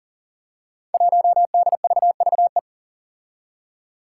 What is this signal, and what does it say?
Morse code, heard as a keyed tone, about 40 wpm, 700 Hz. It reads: 1DVVE
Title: Five Challenging Letter-Number Combinations 40wpm